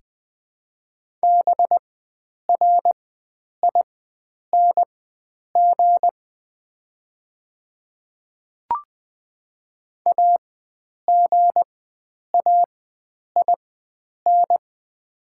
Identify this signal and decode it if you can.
Morse code, heard as a keyed tone, about 20 wpm, spaced as for 9 wpm, 700 Hz. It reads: BRING AGAIN
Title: 4x Character Spacing - Top 500 Words - 20wpm